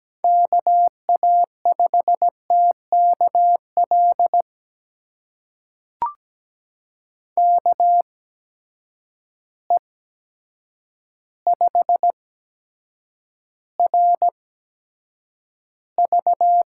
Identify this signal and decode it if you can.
Morse code, heard as a keyed tone, about 17 wpm, spaced as for 8 wpm, 700 Hz. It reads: KA5TKL KE5RV
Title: USA Call Signs - 6 Characters - 8x Character Spacing - 17wpm